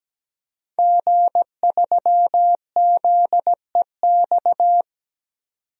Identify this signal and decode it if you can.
Morse code, heard as a keyed tone, about 17 wpm, 700 Hz. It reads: G3ZEX
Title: DXCC-100 Call Signs - 5 Characters - 17wpm